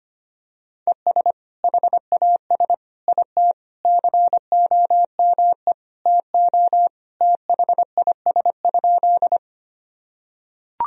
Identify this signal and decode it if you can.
Morse code, heard as a keyed tone, about 25 wpm, 700 Hz. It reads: ES HAS IT COME TO THIS?
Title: Sentences from Top 100 Words and Abbreviations 25wpm